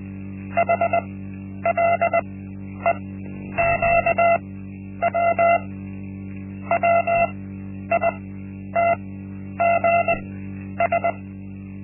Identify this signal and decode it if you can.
Morse code, heard as a keyed tone, about 20 wpm, spaced as for 10 wpm, 680 Hz. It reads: HLEQW WITGS
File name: Linksmakalnis-1